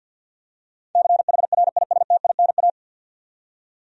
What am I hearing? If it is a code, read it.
Morse code, heard as a keyed tone, about 47 wpm, 700 Hz. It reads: CHRISTINA
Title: USA Common Names - 50wpm